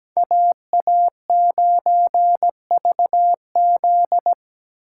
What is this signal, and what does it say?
Morse code, heard as a keyed tone, about 17 wpm, 700 Hz. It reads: AA9VZ
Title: USA Call Signs - 5 Characters - 4x Character Spacing - 17wpm